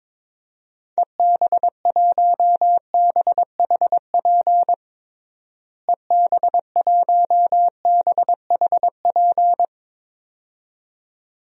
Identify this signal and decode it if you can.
Morse code, heard as a keyed tone, about 22 wpm, 700 Hz. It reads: EB1BHP EB1BHP
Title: DXCC-100 Call Signs - 6 Characters - Repeated 2x - 22wpm